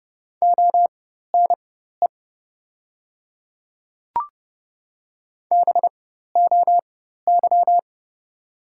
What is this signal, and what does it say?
Morse code, heard as a keyed tone, about 29 wpm, spaced as for 14 wpm, 700 Hz. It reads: ONE BOY
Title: 4x Character Spacing - Top 300 Words - 30wpm